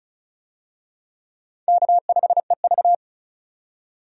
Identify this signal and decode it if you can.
Morse code, heard as a keyed tone, about 35 wpm, 700 Hz. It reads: K5EV
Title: USA Call Signs - 4 Characters - 35wpm